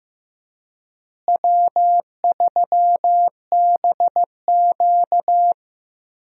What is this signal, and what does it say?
Morse code, heard as a keyed tone, about 15 wpm, 700 Hz. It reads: W3BQ
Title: USA Call Signs - 4 Characters - Repeated 2x - 15wpm